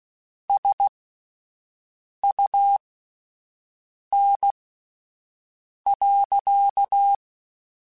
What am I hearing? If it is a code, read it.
Morse code, heard as a keyed tone, about 16 wpm, spaced as for 5 wpm, 800 Hz. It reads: SUN.
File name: Spaced Characters / Spaced Characters-mp3s / Weather5 5 sp @10